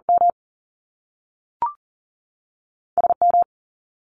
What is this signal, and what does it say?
Morse code, heard as a keyed tone, about 40 wpm, spaced as for 3 wpm, 700 Hz. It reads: MS M
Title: Top Two Letter Combinations in Words 40wpm